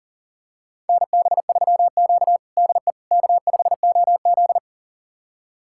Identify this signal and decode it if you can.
Morse code, heard as a keyed tone, about 40 wpm, 700 Hz. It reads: NB3Q DE K5OZ
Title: USA Callsign DE Callsign - 4 Characters - 40wpm